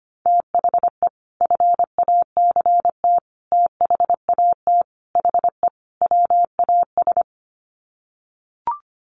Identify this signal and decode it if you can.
Morse code, heard as a keyed tone, about 25 wpm, 700 Hz. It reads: THE FACT THAT HE WAS
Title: Sets of 5 Words 25wpm